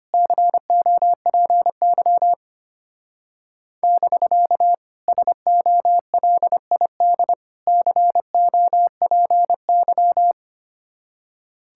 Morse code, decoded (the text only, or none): COPY <BK> SOLID COPY